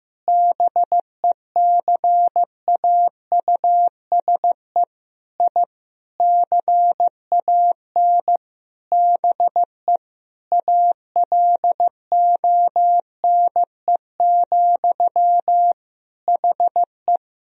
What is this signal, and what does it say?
Morse code, heard as a keyed tone, about 15 wpm, 700 Hz. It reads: BECAUSE I CAN BE ALONE, HE
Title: Sentences from Top 1000 Words - Review Part 3 - 15wpm